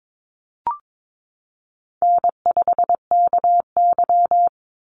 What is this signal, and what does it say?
Morse code, heard as a keyed tone, about 22 wpm, 700 Hz. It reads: N5KY
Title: Easy Call Signs Repeated 3x - 22wpm